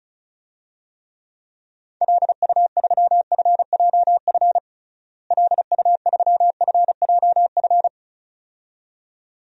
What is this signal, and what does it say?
Morse code, heard as a keyed tone, about 36 wpm, 700 Hz. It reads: LU3FJF LU3FJF
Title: DXCC-100 Call Signs - 6 Characters - Repeated 2x - 35wpm